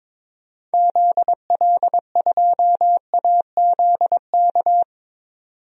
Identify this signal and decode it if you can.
Morse code, heard as a keyed tone, about 22 wpm, 700 Hz. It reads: ZL2AZK
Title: DXCC-100 Call Signs - 6 Characters - 22wpm